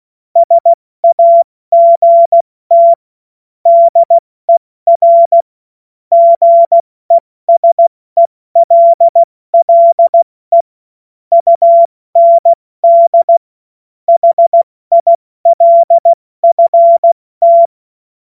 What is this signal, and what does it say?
Morse code, heard as a keyed tone, about 16 wpm, 680 Hz. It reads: SAGT DER GESELLE UND HILFT
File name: Morsekurs / Witze / Witz 20. Januar 2026 70 F80